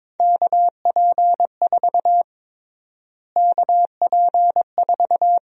Morse code, heard as a keyed tone, about 22 wpm, 700 Hz. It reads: KP4 KP4
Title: USA Call Sign Prefixes - Repeated 2x - 22wpm